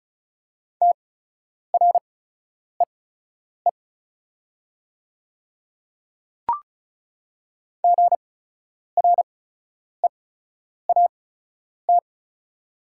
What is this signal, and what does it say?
Morse code, heard as a keyed tone, about 36 wpm, spaced as for 10 wpm, 690 Hz. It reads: TREE GREAT
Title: 8x Character Spacing - Top 500 Words - 35wpm